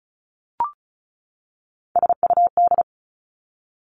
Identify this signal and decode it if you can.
Morse code, heard as a keyed tone, about 36 wpm, 700 Hz. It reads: SUD
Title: Three Challenging Letter-Number Combinations 35wpm